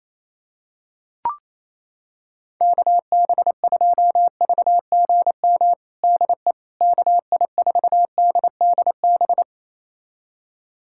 Morse code, heard as a keyed tone, about 28 wpm, 700 Hz. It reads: KB2VGM DE KI4DDB
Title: USA Callsign DE Callsign - 6 Characters - 28wpm